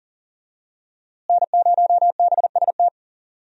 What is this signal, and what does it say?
Morse code, heard as a keyed tone, about 40 wpm, 700 Hz. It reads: N0BST
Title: USA Call Signs - 5 Characters - Repeated 2x - 40wpm